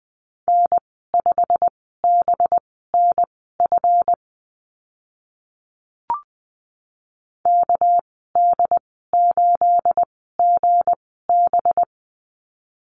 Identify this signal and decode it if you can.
Morse code, heard as a keyed tone, about 20 wpm, spaced as for 14 wpm, 700 Hz. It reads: N5BNF KD8GB
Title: 213 Lesson 34g - Number 8 - Call Signs 20wpm